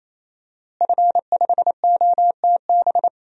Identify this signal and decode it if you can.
Morse code, heard as a keyed tone, about 28 wpm, 700 Hz. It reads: F5OTB
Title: DXCC-100 Call Signs - 5 Characters - Repeated 3x - 28wpm